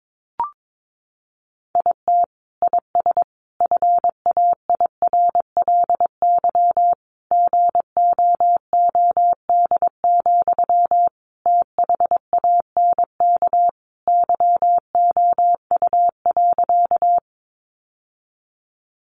Morse code, heard as a keyed tone, about 22 wpm, 700 Hz. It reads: IT IS FAIRLY GOOD, THANK YOU.
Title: Sentences from Top 1100 Words 22wpm